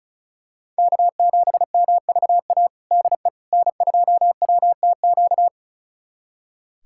Morse code, heard as a keyed tone, about 36 wpm, 700 Hz. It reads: K7MVA DE N2WTQ
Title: USA Callsign DE Callsign - 5 Characters - 35wpm